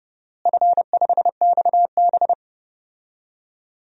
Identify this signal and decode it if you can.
Morse code, heard as a keyed tone, about 29 wpm, 700 Hz. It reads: F5XB
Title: DXCC-100 Call Signs - 4 Characters - Repeated 3x - 30wpm